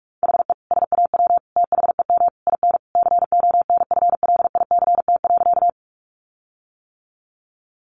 Morse code, heard as a keyed tone, about 44 wpm, 700 Hz. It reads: HE SAW THEM IN CONFLICT.
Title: Sentences from Top 1300 Words 45wpm